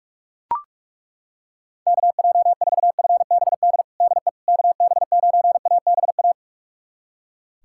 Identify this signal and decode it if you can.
Morse code, heard as a keyed tone, about 41 wpm, 700 Hz. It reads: KJ4FBD DE KB9ABA
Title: USA Callsign DE Callsign - 6 Characters - 45wpm